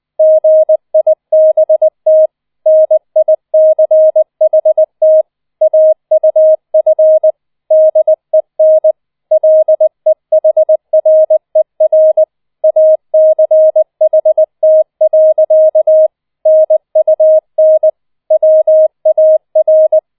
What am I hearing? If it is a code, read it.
Morse code, heard as a keyed tone, about 20 wpm, 630 Hz. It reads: GIBT NICHT AUF DEN LEHRER ACHT. NUN WAR